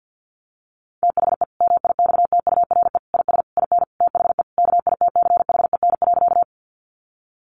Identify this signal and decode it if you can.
Morse code, heard as a keyed tone, about 50 wpm, 700 Hz. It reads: THE MIXTURE IS IN THE KITCHEN.
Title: Sentences from Top 1500 Words 50wpm